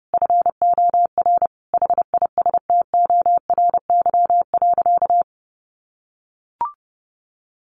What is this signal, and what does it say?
Morse code, heard as a keyed tone, about 30 wpm, 700 Hz. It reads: FOR HISTORY.
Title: Sentences from Top 600 Words 30wpm